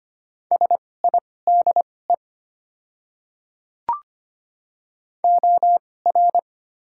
Morse code, heard as a keyed tone, about 25 wpm, spaced as for 18 wpm, 700 Hz. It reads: SIDE OR
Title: 2x Character Spacing - Top 100 Words - 25wpm